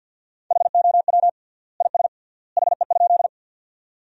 Morse code, heard as a keyed tone, about 50 wpm, 700 Hz. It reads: HOW IS HE?
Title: 3x Word-Spacing - Sentences from Top 100 Words - 50wpm